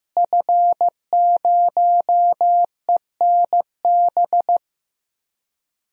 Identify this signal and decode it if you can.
Morse code, heard as a keyed tone, about 15 wpm, 700 Hz. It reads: F0ENB
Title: DXCC-100 Call Signs - 5 Characters - Repeated 3x - 15wpm